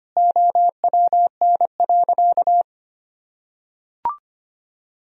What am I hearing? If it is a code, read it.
Morse code, heard as a keyed tone, about 25 wpm, 700 Hz. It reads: OWN.
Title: Sentences from Top 1000 Words - Review Part 2 - 25wpm